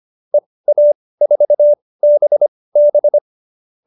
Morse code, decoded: EA4BB

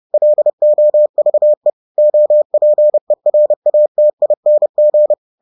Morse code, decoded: LOVE OPERATING

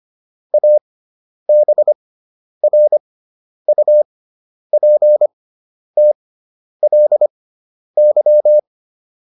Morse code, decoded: ABRUPTLY

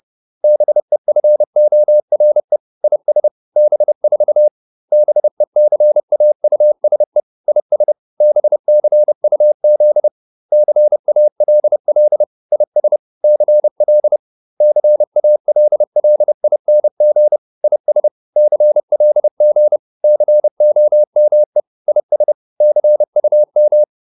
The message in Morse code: BEFORE IS B4 BECAUSE IS BCUZ CALL IS CL CALLING IS CLG COME IS CUM

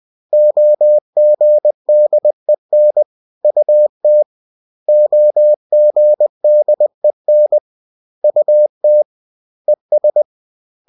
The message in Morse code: OGDEN UT OGDEN UT ES

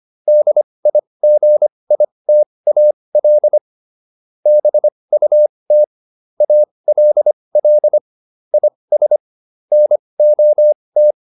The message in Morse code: DIGITAL BUT ALL IS NOT

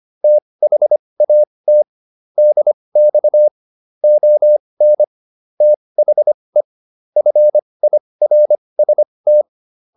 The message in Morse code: THAT DX ON THE FIRST

